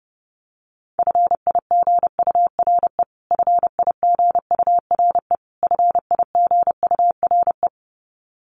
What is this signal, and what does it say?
Morse code, heard as a keyed tone, about 28 wpm, 700 Hz. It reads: FIGURE FIGURE FIGURE
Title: Top 500 Words Repeated 30wpm